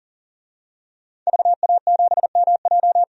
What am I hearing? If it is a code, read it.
Morse code, heard as a keyed tone, about 40 wpm, 700 Hz. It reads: VA7MJ
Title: DXCC-100 Call Signs - 5 Characters - 40wpm